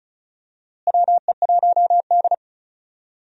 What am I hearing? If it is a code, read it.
Morse code, heard as a keyed tone, about 34 wpm, 700 Hz. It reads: WE1D
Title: USA Call Signs - 4 Characters - Repeated 3x - 35wpm